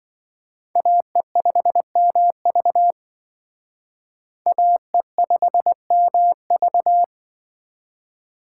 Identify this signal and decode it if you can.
Morse code, heard as a keyed tone, about 25 wpm, 700 Hz. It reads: AE5MV AE5MV